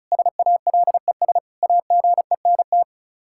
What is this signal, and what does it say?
Morse code, heard as a keyed tone, about 34 wpm, 700 Hz. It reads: SALES AGENT
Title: Practice QSO - Exchange 3 - More Info - 35wpm